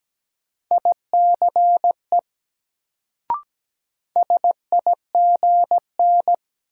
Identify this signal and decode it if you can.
Morse code, heard as a keyed tone, about 17 wpm, 700 Hz. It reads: ICE SIGN
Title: Sentences from Top 1000 Words - New Word Drill - 17wpm